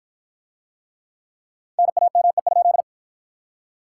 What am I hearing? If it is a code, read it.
Morse code, heard as a keyed tone, about 54 wpm, 700 Hz. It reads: NAME?